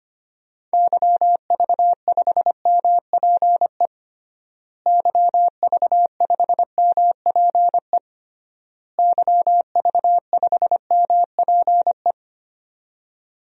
Morse code, heard as a keyed tone, about 25 wpm, 700 Hz. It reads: YV5MPE YV5MPE YV5MPE
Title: DXCC-100 Call Signs - 6 Characters - Repeated 3x - 25wpm